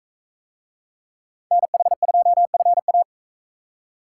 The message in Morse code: NH2FA

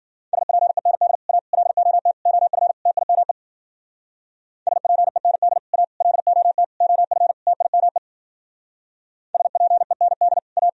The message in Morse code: SPEND A LOT OF TIME SPEND A LOT OF TIME SPEND A

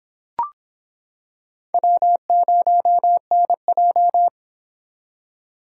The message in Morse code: W0NJ

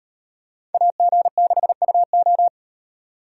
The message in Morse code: AG6UO

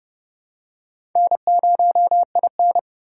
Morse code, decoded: N0IN

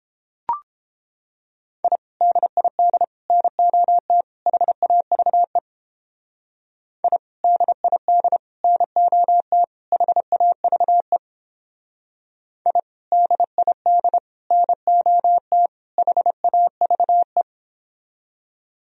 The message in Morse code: I DID NOT HAVE I DID NOT HAVE I DID NOT HAVE

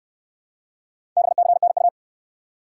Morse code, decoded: BCNU

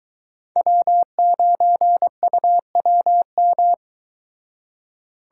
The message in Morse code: W9UWM

